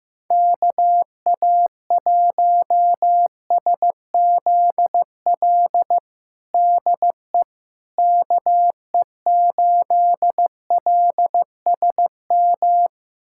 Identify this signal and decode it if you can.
Morse code, heard as a keyed tone, about 16 wpm, 700 Hz. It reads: KA1SZL DE KE8LSM
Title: USA Callsign DE Callsign - 6 Characters - 15wpm